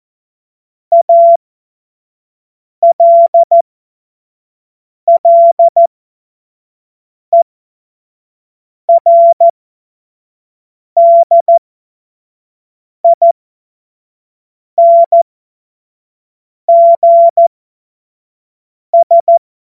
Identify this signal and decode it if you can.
Morse code, heard as a keyed tone, about 14 wpm, spaced as for 5 wpm, 680 Hz. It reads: ALLERDINGS
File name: Morsekurs / Witze / Witz 13. Juli 2021, 25 F70